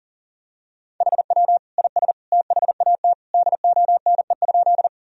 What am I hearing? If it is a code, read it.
Morse code, heard as a keyed tone, about 40 wpm, 700 Hz. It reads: HW IS THAT DONE?
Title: Sentences from Top 500 Words and Abbreviations 40wpm